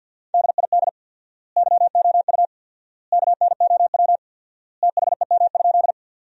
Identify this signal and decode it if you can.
Morse code, heard as a keyed tone, about 50 wpm, 700 Hz. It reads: DID YOU KNOW THEM?
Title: 4x Word-Spacing - Sentences from Top 100 Words - 50wpm